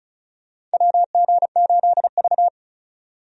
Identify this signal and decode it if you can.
Morse code, heard as a keyed tone, about 34 wpm, 700 Hz. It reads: WG8V
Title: USA Call Signs - 4 Characters - 35wpm